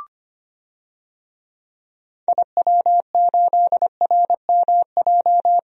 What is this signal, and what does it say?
Morse code, heard as a keyed tone, about 25 wpm, 700 Hz. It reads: IW8RMJ
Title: DXCC-100 Call Signs - 6 Characters - Repeated 2x - 25wpm